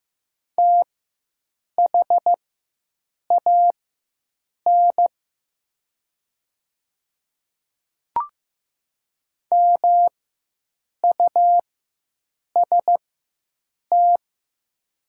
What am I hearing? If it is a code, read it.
Morse code, heard as a keyed tone, about 15 wpm, spaced as for 7 wpm, 700 Hz. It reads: THAN MUST
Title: 4x Character Spacing - Top 200 Words - 15wpm